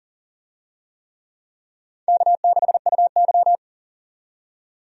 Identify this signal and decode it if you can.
Morse code, heard as a keyed tone, about 40 wpm, 700 Hz. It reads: K6UY
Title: USA Call Signs - 4 Characters - 40wpm